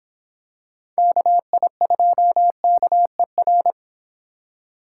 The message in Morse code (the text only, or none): KI2KER